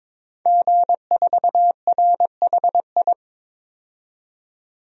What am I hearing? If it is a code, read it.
Morse code, heard as a keyed tone, about 22 wpm, 700 Hz. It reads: G4RHI
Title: DXCC-100 Call Signs - 5 Characters - 22wpm